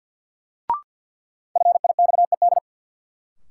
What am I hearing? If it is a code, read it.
Morse code, heard as a keyed tone, about 52 wpm, 700 Hz. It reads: FIXED
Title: Sentences from Top 1100 Words - New Word Drill Alt - 50wpm